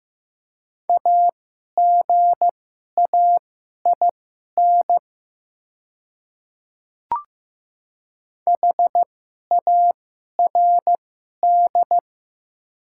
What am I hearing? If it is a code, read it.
Morse code, heard as a keyed tone, about 15 wpm, spaced as for 11 wpm, 700 Hz. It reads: AGAIN HARD
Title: 2x Character Spacing - Top 300 Words - 15wpm